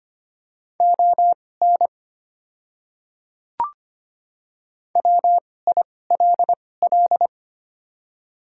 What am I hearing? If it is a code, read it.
Morse code, heard as a keyed tone, about 25 wpm, spaced as for 18 wpm, 690 Hz. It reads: ON WILL
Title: 2x Character Spacing - Top 200 Words - 25wpm